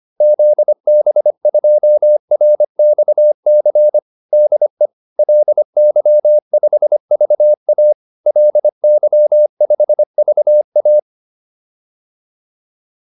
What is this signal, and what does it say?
Morse code, heard as a keyed tone, about 25 wpm, 600 Hz. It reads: ZB2RXC DE LY5VA LY5VA